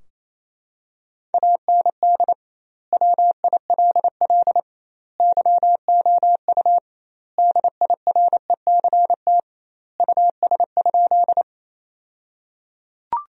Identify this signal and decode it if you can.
Morse code, heard as a keyed tone, about 28 wpm, 700 Hz. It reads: AND WILL YOU DIRECT US?
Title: 2x Word-Spacing - Sentences from Top 400 Words - 28wpm